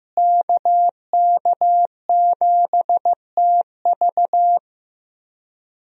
Morse code, heard as a keyed tone, about 15 wpm, 700 Hz. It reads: KK7TV
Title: USA Call Signs - 5 Characters - Repeated 3x - 15wpm